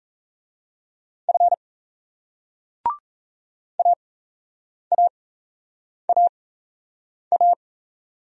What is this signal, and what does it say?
Morse code, heard as a keyed tone, about 38 wpm, spaced as for 8 wpm, 700 Hz. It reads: F AAAA